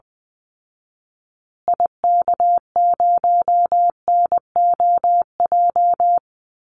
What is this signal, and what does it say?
Morse code, heard as a keyed tone, about 20 wpm, 700 Hz. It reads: IK0NOJ
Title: Call Signs Repeated 2x - 20wpm